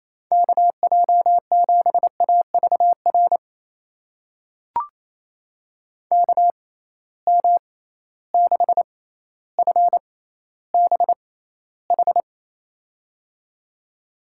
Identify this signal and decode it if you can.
Morse code, heard as a keyed tone, about 28 wpm, spaced as for 14 wpm, 700 Hz. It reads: KJ7AVR KM6FBH
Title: USA Call Signs - 6 Characters - 6x Character Spacing - 28wpm